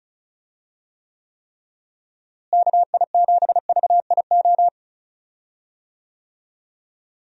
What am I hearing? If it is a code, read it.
Morse code, heard as a keyed tone, about 34 wpm, 700 Hz. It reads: KI7VIO
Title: USA Call Signs - 6 Characters - Repeated 3x - 35wpm